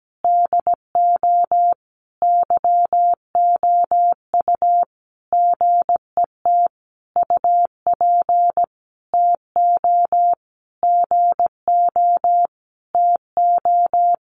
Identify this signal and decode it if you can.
Morse code, heard as a keyed tone, about 17 wpm, 700 Hz. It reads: DO YOU GET UP TO GO TO
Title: Sentences from Top 1500 Words 17wpm